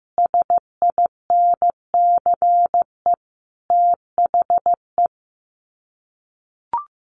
Morse code, heard as a keyed tone, about 15 wpm, 700 Hz. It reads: SINCE THE
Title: Sets of 5 Words Encore 15wpm